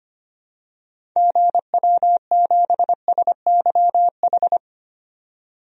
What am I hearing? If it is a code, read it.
Morse code, heard as a keyed tone, about 25 wpm, 700 Hz. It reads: GW7SYH
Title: DXCC-100 Call Signs - 6 Characters - 25wpm